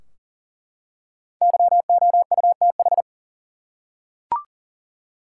Morse code, heard as a keyed tone, about 40 wpm, 700 Hz. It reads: YOUTH
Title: Five Letter Words - 40wpm